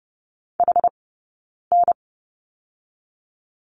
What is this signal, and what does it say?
Morse code, heard as a keyed tone, about 30 wpm, 700 Hz. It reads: H N
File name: 092 Lesson 16d - Letter c - Two Characters 30wpm